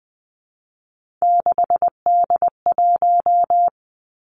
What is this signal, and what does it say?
Morse code, heard as a keyed tone, about 20 wpm, 700 Hz. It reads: 6D1
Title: Three Challenging Letter-Number Combinations 20wpm